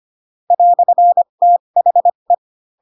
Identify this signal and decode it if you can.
Morse code, heard as a keyed tone, about 25 wpm, spaced as for 22 wpm, 700 Hz. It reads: "THE
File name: Pinocchio / Pinocchio ch7 18 25 ik1hge